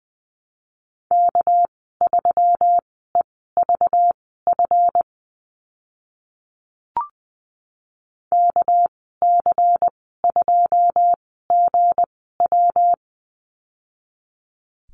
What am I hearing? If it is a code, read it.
Morse code, heard as a keyed tone, about 20 wpm, spaced as for 14 wpm, 700 Hz. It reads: K3EVF KC2GW
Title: 213 Lesson 34g - Number 8 - Call Signs 20wpm